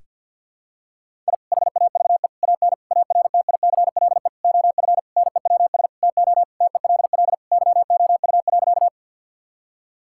Morse code, heard as a keyed tone, about 51 wpm, 690 Hz. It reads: I HAVE AN ARTICLE OF NEWS TO TELL YOU.
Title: Sentences from Top 1000 Words - Review Part 5 - 50wpm